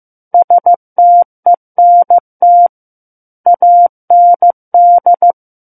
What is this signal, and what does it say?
Morse code, heard as a keyed tone, about 15 wpm, 700 Hz. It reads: STENT AND
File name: Book One / 06-Chapter 6 - 15 WPM